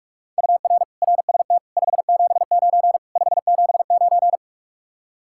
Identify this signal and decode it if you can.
Morse code, heard as a keyed tone, about 48 wpm, 700 Hz. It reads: UR RST 579 579